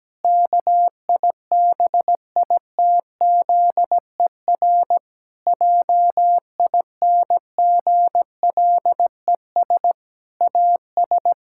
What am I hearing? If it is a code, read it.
Morse code, heard as a keyed tone, about 17 wpm, 700 Hz. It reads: KIBITZER JINGLES AS
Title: Pangram Challenge v2 - 17wpm